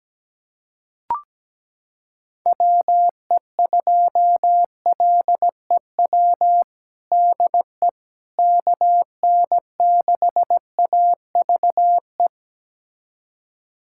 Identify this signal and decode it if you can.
Morse code, heard as a keyed tone, about 17 wpm, 700 Hz. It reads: WE2LEW DE KN6AVE
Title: USA Callsign DE Callsign - 6 Characters - 17wpm